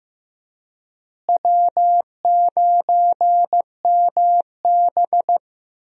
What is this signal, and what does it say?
Morse code, heard as a keyed tone, about 15 wpm, 700 Hz. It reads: W9MB
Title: USA Call Signs - 4 Characters - Repeated 3x - 15wpm